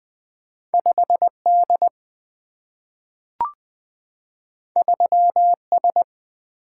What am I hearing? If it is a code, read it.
Morse code, heard as a keyed tone, about 20 wpm, 700 Hz. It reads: 5 D 3 S
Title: Single-Letter Single-Number 20wpm